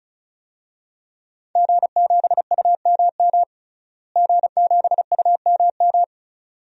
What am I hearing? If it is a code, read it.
Morse code, heard as a keyed tone, about 34 wpm, 700 Hz. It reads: G7UMM G7UMM
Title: DXCC-100 Call Signs - 5 Characters - Repeated 2x - 35wpm